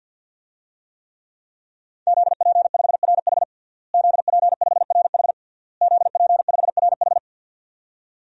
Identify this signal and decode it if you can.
Morse code, heard as a keyed tone, about 52 wpm, 700 Hz. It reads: ZP5RH ZP5RH ZP5RH
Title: DXCC-100 Call Signs - 5 Characters - Repeated 3x - 50wpm